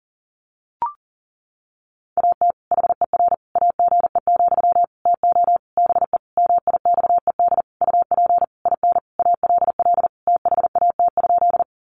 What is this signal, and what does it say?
Morse code, heard as a keyed tone, about 40 wpm, 700 Hz. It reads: AT HER AGE, TO BE MIXED UP IN ALL THAT?
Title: Sentences from Top 1500 Words 40wpm